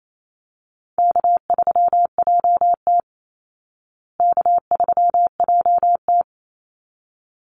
Morse code, heard as a keyed tone, about 28 wpm, 700 Hz. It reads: K3JT K3JT
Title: Easy Call Signs Repeated 2x - 28wpm